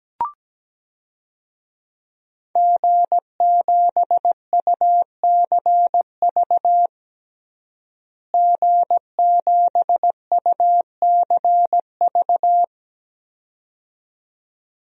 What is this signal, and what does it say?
Morse code, heard as a keyed tone, about 17 wpm, 700 Hz. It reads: G7UCV G7UCV
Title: DXCC-100 Call Signs - 5 Characters - Repeated 2x - 17wpm